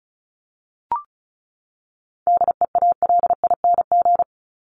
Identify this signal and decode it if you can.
Morse code, heard as a keyed tone, about 34 wpm, 700 Hz. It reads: DEALING
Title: ING Suffix - Top 100 Words with Base Words 35wpm